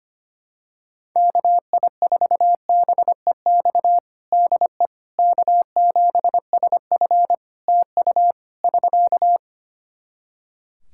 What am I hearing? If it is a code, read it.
Morse code, heard as a keyed tone, about 25 wpm, 700 Hz. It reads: KI4BEX DE K7SF TU <SK>
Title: Practice QSO - Exchange 4 - Ending - 25wpm